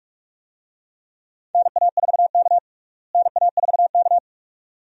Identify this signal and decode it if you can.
Morse code, heard as a keyed tone, about 46 wpm, 700 Hz. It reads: NA4K NA4K
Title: USA Call Signs - 4 Characters - Repeated 2x - 45wpm